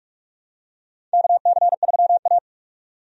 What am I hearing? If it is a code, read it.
Morse code, heard as a keyed tone, about 47 wpm, 700 Hz. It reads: KC3A
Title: USA Call Signs - 4 Characters - 45wpm